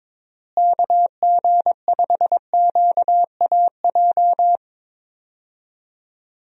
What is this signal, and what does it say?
Morse code, heard as a keyed tone, about 22 wpm, 700 Hz. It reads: KG5QAJ
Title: USA Call Signs - 6 Characters - Repeated 2x - 22wpm